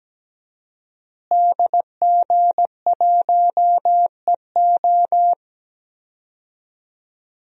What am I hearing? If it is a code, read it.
Morse code, heard as a keyed tone, about 17 wpm, 700 Hz. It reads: DG1EO
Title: DXCC-100 Call Signs - 5 Characters - Repeated 3x - 17wpm